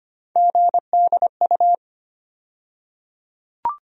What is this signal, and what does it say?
Morse code, heard as a keyed tone, about 25 wpm, 700 Hz. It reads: GDU
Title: USA Call Sign Postfixes - 3 Letters 25wpm